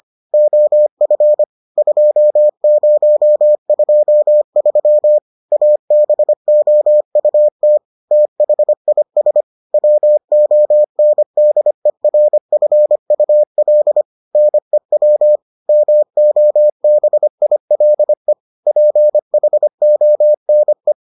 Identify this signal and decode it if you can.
Morse code, heard as a keyed tone, about 25 wpm, 600 Hz. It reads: OF 2023 ABOUT THIS WONDERFUL NEW MOBILE PHONE